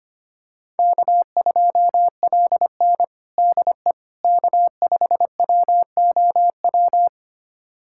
Morse code, heard as a keyed tone, about 25 wpm, 700 Hz. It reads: K2LN DE K5WOW
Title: USA Callsign DE Callsign 25wpm